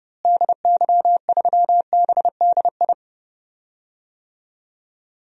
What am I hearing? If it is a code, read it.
Morse code, heard as a keyed tone, about 30 wpm, 700 Hz. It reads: DY3BDI
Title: DXCC-100 Call Signs - 6 Characters - Repeated 3x - 30wpm